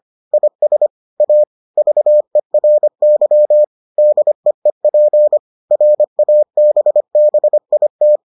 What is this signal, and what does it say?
Morse code, heard as a keyed tone, about 25 wpm, 600 Hz. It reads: IS A VERY DEEP RABBIT